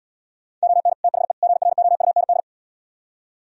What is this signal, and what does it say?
Morse code, heard as a keyed tone, about 77 wpm, 700 Hz. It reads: ON THE GROUND